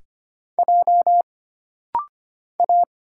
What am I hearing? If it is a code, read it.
Morse code, heard as a keyed tone, about 25 wpm, 700 Hz. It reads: J A
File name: Single Character (Letter:Number:Punctuation:Prosign) - Rapid-Fire - 25wpm